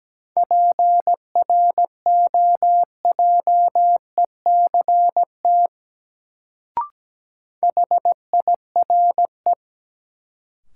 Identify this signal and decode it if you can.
Morse code, heard as a keyed tone, about 17 wpm, 700 Hz. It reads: PROJECT HIRE
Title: Sentences from Top 1400 Words - New Word Drill - 17wpm